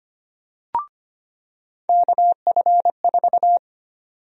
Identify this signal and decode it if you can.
Morse code, heard as a keyed tone, about 25 wpm, 700 Hz. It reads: KF4
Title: USA Call Sign Prefixes - 25wpm